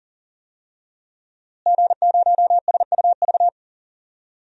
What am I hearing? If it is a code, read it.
Morse code, heard as a keyed tone, about 37 wpm, 700 Hz. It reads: G0SUV
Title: DXCC-100 Call Signs - 5 Characters - Repeated 2x - 40wpm